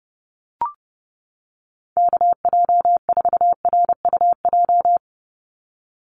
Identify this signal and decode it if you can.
Morse code, heard as a keyed tone, about 30 wpm, 700 Hz. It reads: KJ4RUJ
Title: USA Call Signs 30wpm